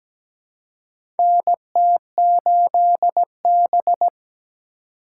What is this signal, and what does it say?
Morse code, heard as a keyed tone, about 17 wpm, 700 Hz. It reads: NT8B
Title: USA Call Signs - 4 Characters - 4x Character Spacing - 17wpm